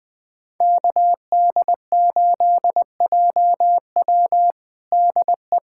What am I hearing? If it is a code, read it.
Morse code, heard as a keyed tone, about 20 wpm, 700 Hz. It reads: KD8JW DE
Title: USA Callsign DE Callsign - 5 Characters - 20wpm